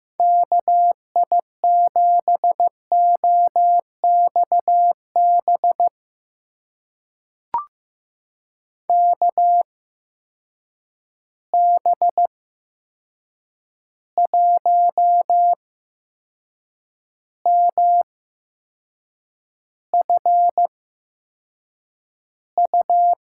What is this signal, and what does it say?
Morse code, heard as a keyed tone, about 15 wpm, spaced as for 6 wpm, 700 Hz. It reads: KI7OXB KB1MFU
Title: USA Call Signs - 6 Characters - 8x Character Spacing - 15wpm